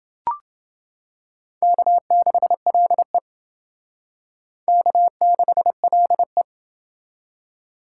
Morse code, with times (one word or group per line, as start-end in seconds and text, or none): 1.6-3.2 K6LE
4.7-6.4 K6LE